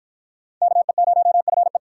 Morse code, heard as a keyed tone, about 46 wpm, 700 Hz. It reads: KE0FE